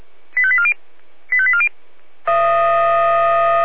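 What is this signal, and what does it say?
Five-tone selective call (ZVEI), heard as quick runs of five tones, tones of about 70 ms. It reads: 85840 85840
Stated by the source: ZVEI five-tone selective call received via SDR.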